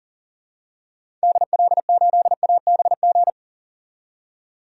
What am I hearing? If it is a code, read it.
Morse code, heard as a keyed tone, about 40 wpm, 700 Hz. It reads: DL8ABG
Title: DXCC-100 Call Signs - 6 Characters - 40wpm